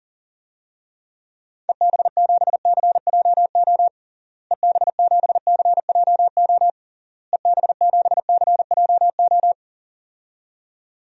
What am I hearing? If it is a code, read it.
Morse code, heard as a keyed tone, about 40 wpm, 700 Hz. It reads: EB7CJO EB7CJO EB7CJO
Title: DXCC-100 Call Signs - 6 Characters - Repeated 3x - 40wpm